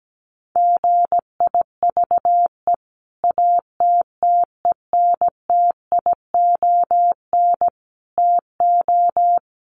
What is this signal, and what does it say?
Morse code, heard as a keyed tone, about 17 wpm, 700 Hz. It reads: GIVE ATTENTION TO
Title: Sentences from Top 1000 Words 17wpm